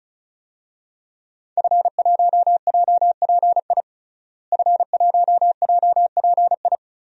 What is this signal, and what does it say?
Morse code, heard as a keyed tone, about 36 wpm, 700 Hz. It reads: F1JPI F1JPI
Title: DXCC-100 Call Signs - 5 Characters - Repeated 2x - 35wpm